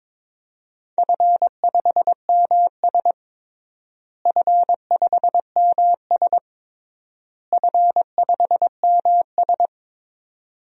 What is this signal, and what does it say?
Morse code, heard as a keyed tone, about 22 wpm, 700 Hz. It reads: F5MS F5MS F5MS
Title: DXCC-100 Call Signs - 4 Characters - Repeated 3x - 22wpm